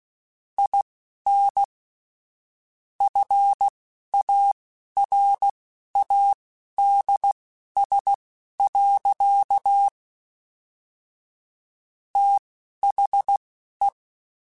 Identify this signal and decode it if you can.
Morse code, heard as a keyed tone, about 16 wpm, spaced as for 11 wpm, 800 Hz. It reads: IN FARADS. THE
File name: Technical Subjects / Technical Subjects-mp3s / XC @10